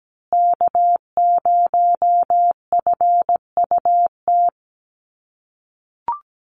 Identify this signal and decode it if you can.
Morse code, heard as a keyed tone, about 17 wpm, 700 Hz. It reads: K0FUT
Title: USA Call Signs Repeated 3x 17wpm